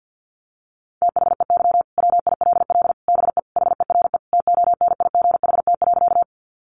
Morse code, heard as a keyed tone, about 48 wpm, 700 Hz. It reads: THEY WILL BE HERE TONIGHT.
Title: Sentences from Top 600 Words 50wpm